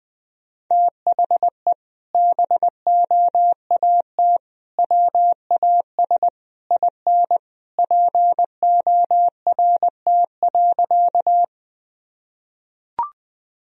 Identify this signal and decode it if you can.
Morse code, heard as a keyed tone, about 20 wpm, 700 Hz. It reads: THE BOAT WAS IN PORT.
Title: Sentences from Top 1000 Words - Review Part 1 - 20wpm